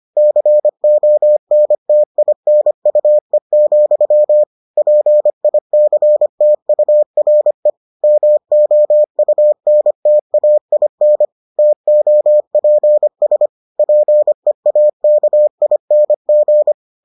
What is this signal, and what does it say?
Morse code, heard as a keyed tone, about 25 wpm, 600 Hz. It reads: CONTINUE, PICTURE MOUNTAIN TOPS PEAKING